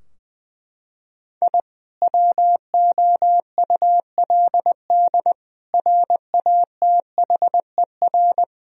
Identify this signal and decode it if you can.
Morse code, heard as a keyed tone, about 20 wpm, 700 Hz. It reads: I WOULD RATHER
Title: Sentences from Top 1000 Words - Review Part 4 - 20wpm